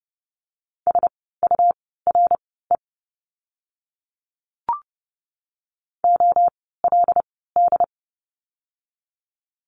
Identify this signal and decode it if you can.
Morse code, heard as a keyed tone, about 30 wpm, spaced as for 17 wpm, 700 Hz. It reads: SURE OLD
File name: Lesson 17f - Letter u - Words 30wpm